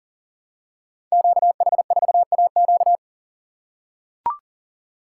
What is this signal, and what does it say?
Morse code, heard as a keyed tone, about 40 wpm, 700 Hz. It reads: QH4AQ
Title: Code Groups - 5 Letter:Numbers - 40wpm